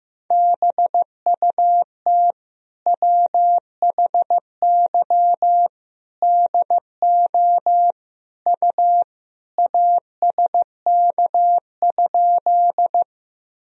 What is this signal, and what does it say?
Morse code, heard as a keyed tone, about 15 wpm, 700 Hz. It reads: BUT WHY DO U ASK?
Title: Sentences from Top 200 Words and Abbreviations 15wpm